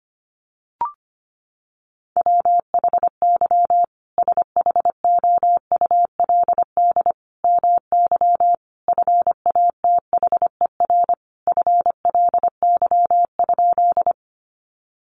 Morse code, heard as a keyed tone, about 25 wpm, 700 Hz. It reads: WHY SHOULD MY FATHER FLY?
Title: Sentences from Top 500 Words 25wpm